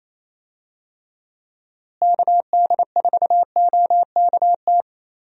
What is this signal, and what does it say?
Morse code, heard as a keyed tone, about 28 wpm, 700 Hz. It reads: KD4OKT
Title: USA Call Signs - 6 Characters - Repeated 3x - 28wpm